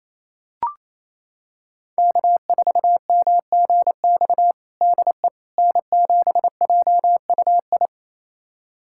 Morse code, heard as a keyed tone, about 28 wpm, 700 Hz. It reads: K4MGX DE N7JUI
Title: USA Callsign DE Callsign - 5 Characters - 28wpm